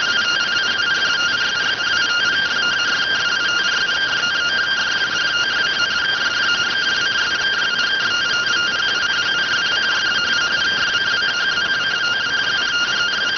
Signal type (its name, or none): two-tone teleprinter signal